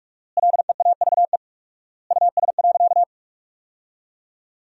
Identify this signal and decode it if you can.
Morse code, heard as a keyed tone, about 46 wpm, 700 Hz. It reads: LEAVE US.
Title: 4x Word-Spacing - Sentences from Top 400 Words - 45wpm